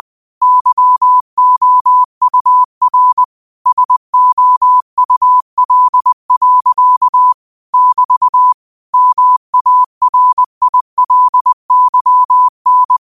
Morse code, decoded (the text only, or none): YOUR SOUL. <BT> MARILYN